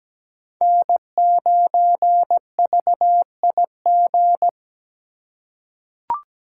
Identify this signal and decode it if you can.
Morse code, heard as a keyed tone, about 17 wpm, 700 Hz. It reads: N9VIG
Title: USA Call Signs - 5 Characters - 3x Character Spacing - 17wpm